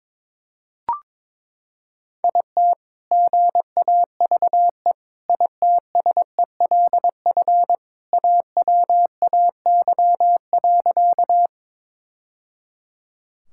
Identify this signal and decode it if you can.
Morse code, heard as a keyed tone, about 22 wpm, 700 Hz. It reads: IT GAVE ITSELF AWAY.
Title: Sentences from Top 1000 Words - Review Part 5 - 22wpm